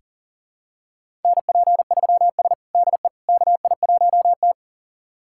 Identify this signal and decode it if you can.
Morse code, heard as a keyed tone, about 39 wpm, 700 Hz. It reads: NP3S DE KI1T
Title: USA Callsign DE Callsign - 4 Characters - 40wpm